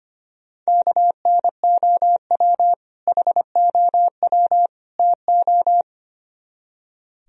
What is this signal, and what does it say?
Morse code, heard as a keyed tone, about 32 wpm, 700 Hz. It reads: KNOW HOW TO